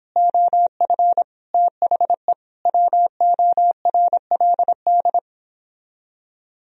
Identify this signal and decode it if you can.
Morse code, heard as a keyed tone, about 29 wpm, 700 Hz. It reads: OF THE WORLD